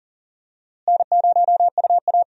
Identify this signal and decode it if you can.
Morse code, heard as a keyed tone, about 39 wpm, 700 Hz. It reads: N0UA
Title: USA Call Signs - 4 Characters - Repeated 3x - 40wpm